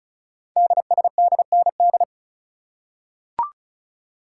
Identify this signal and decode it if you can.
Morse code, heard as a keyed tone, about 36 wpm, 700 Hz. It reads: DSDND
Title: Code Groups - 5 Letters - 35wpm